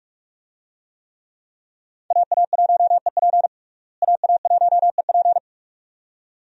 Morse code, heard as a keyed tone, about 46 wpm, 700 Hz. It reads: AA1EP AA1EP
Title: USA Call Signs - 5 Characters - Repeated 2x - 45wpm